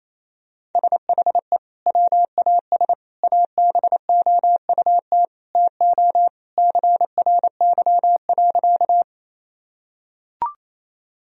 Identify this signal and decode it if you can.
Morse code, heard as a keyed tone, about 28 wpm, 700 Hz. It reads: SHE WAS ABOUT TO CRY.
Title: Sentences from Top 1000 Words - Review Part 2 - 28wpm